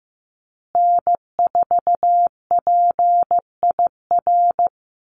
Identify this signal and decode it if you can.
Morse code, heard as a keyed tone, about 15 wpm, 700 Hz. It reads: N4PIR
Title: Call Signs Repeated 2x - 15wpm